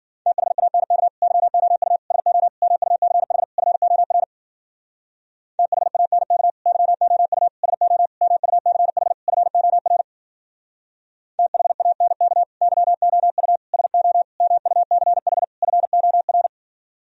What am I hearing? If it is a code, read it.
Morse code, heard as a keyed tone, about 51 wpm, 700 Hz. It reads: THANK YOU SO MUCH FOR THANK YOU SO MUCH FOR THANK YOU SO MUCH FOR